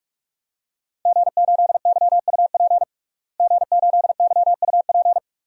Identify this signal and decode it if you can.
Morse code, heard as a keyed tone, about 46 wpm, 700 Hz. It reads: G8YUP G8YUP
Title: DXCC-100 Call Signs - 5 Characters - Repeated 2x - 45wpm